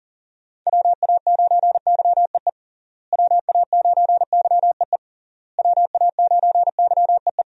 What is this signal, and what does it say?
Morse code, heard as a keyed tone, about 40 wpm, 700 Hz. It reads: WA9YEE WA9YEE WA9YEE
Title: USA Call Signs - 6 Characters - Repeated 3x - 40wpm